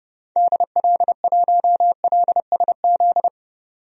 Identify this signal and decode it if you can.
Morse code, heard as a keyed tone, about 30 wpm, 700 Hz. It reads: DL1LSZ
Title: DXCC-100 Call Signs - 6 Characters - 30wpm